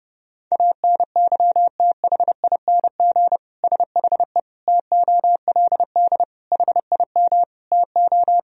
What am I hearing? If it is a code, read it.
Morse code, heard as a keyed tone, about 30 wpm, 700 Hz. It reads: ANYTHING SHE TOLD HIM TO
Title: Sentences from Top 1000 Words - Review Part 3 - 30wpm